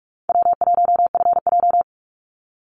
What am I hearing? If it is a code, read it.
Morse code, heard as a keyed tone, about 44 wpm, 700 Hz. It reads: W1FJ
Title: Easy Call Signs - 45wpm